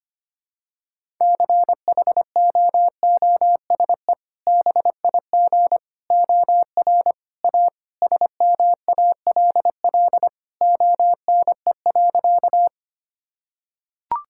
Morse code, heard as a keyed tone, about 25 wpm, 700 Hz. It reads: CHOOSE BIG OR A SMALL ONE.
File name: practice / 25wpm / Sentences with 2-Word Phrases - 25wpm